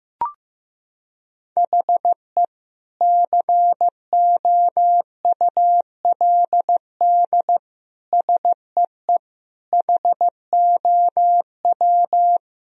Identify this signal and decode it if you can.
Morse code, heard as a keyed tone, about 15 wpm, 700 Hz. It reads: HE COULD SEE HOW
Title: Sentences from Top 1000 Words - Review Part 3 - 15wpm